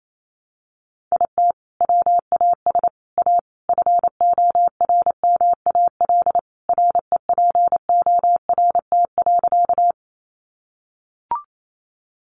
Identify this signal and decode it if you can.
Morse code, heard as a keyed tone, about 28 wpm, 700 Hz. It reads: IT WAS A FORMAL REPORT.
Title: Sentences from Top 1300 Words 28wpm